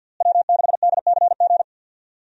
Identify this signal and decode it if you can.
Morse code, heard as a keyed tone, about 47 wpm, 700 Hz. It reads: W6DCG
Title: USA Call Signs - 5 Characters - Repeated 2x - 50wpm